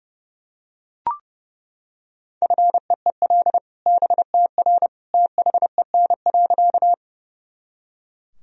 Morse code, heard as a keyed tone, about 30 wpm, 700 Hz. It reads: FEEL BTR THEN.
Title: Sentences from Top 400 Words and Abbreviations 30wpm